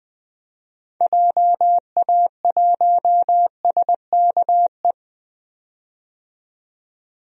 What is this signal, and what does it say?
Morse code, heard as a keyed tone, about 20 wpm, 700 Hz. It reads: JA1SKE
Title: DXCC-100 Call Signs - 6 Characters - 20wpm